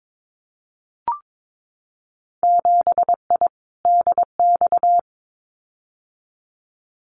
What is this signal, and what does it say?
Morse code, heard as a keyed tone, about 22 wpm, 700 Hz. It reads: 7I DX
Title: Winter Field Day Exchanges Encore 22wpm